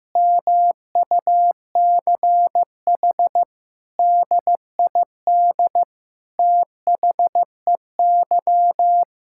Morse code, heard as a keyed tone, about 16 wpm, 700 Hz. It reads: MUCH DID THEY
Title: Sentences from Top 1000 Words - Review Part 6 - 15wpm